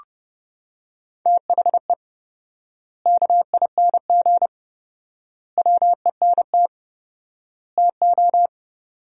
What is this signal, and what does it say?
Morse code, heard as a keyed tone, about 30 wpm, 700 Hz. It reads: THE KING WENT TO
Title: 4x Word-Spacing - Sentences from Top 400 Words - 30wpm